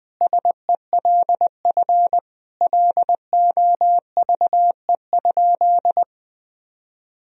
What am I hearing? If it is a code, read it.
Morse code, heard as a keyed tone, about 20 wpm, 700 Hz. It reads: SELF LOVE?
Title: Sentences from Top 1000 Words - Review Part 1 - 20wpm